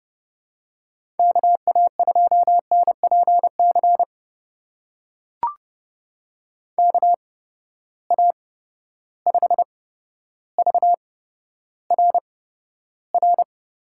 Morse code, heard as a keyed tone, about 30 wpm, spaced as for 13 wpm, 700 Hz. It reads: KA2NPC KA5VRR
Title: USA Call Signs - 6 Characters - 8x Character Spacing - 30wpm